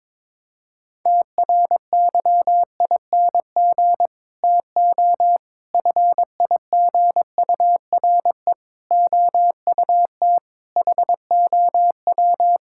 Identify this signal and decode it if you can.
Morse code, heard as a keyed tone, about 21 wpm, 700 Hz. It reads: TRYING TO FIGURE OUT HOW
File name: practice / 22wpm / Sets of 6 Words - 22wpm